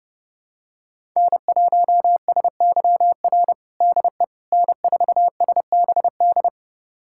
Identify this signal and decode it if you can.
Morse code, heard as a keyed tone, about 32 wpm, 700 Hz. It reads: N1SYR DE N4SBD
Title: USA Callsign DE Callsign - 5 Characters - 30wpm